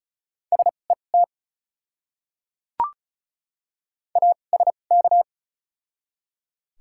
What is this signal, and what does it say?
Morse code, heard as a keyed tone, about 36 wpm, spaced as for 25 wpm, 700 Hz. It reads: SET ASK
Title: 2x Character Spacing - Top 200 Words - 35wpm